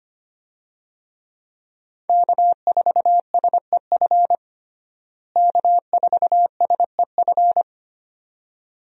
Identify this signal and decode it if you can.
Morse code, heard as a keyed tone, about 25 wpm, 700 Hz. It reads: K4SEF K4SEF
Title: USA Call Signs - 5 Characters - Repeated 2x - 25wpm